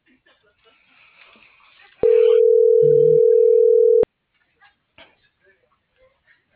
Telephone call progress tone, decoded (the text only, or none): ringback tone